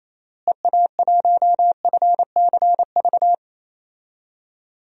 Morse code, heard as a keyed tone, about 28 wpm, 700 Hz. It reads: EA1FCV
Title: DXCC-100 Call Signs - 6 Characters - Repeated 2x - 28wpm